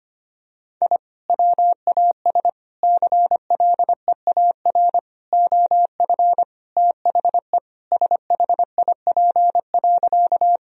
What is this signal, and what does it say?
Morse code, heard as a keyed tone, about 25 wpm, 700 Hz. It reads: I WAS CLEAR OF THE SHIP.
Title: Sentences from Top 1000 Words - Review Part 2 - 25wpm